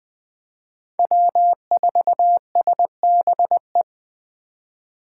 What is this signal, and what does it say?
Morse code, heard as a keyed tone, about 20 wpm, 700 Hz. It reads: W4SBE
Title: USA Call Signs - 5 Characters - 3x Character Spacing - 20wpm